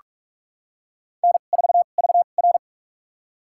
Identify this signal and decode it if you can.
Morse code, heard as a keyed tone, about 46 wpm, spaced as for 33 wpm, 690 Hz. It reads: N4VR
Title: USA Call Signs - 4 Characters - 2x Character Spacing - 45wpm